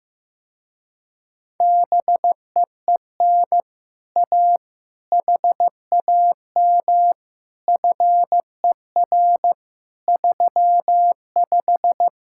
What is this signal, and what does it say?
Morse code, heard as a keyed tone, about 15 wpm, 700 Hz. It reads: BEEN A HAM FER 35
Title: Speed-Racing Common QSO Phrases Encore - 15wpm